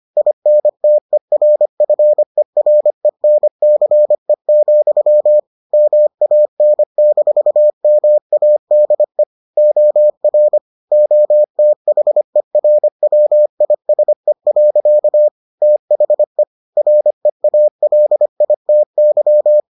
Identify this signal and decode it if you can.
Morse code, heard as a keyed tone, about 25 wpm, 600 Hz. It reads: INTERFERENCE, MAN-MADE OR OTHERWISE. THE REALITY